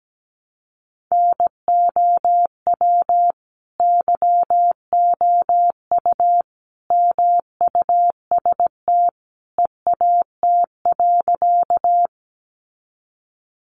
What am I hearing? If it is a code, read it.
Morse code, heard as a keyed tone, about 17 wpm, 700 Hz. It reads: NOW YOU MUST EAT.
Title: Sentences from Top 300 Words 17wpm